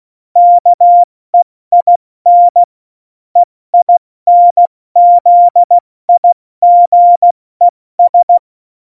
Morse code, heard as a keyed tone, about 16 wpm, 700 Hz. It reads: KEIN EINZIGES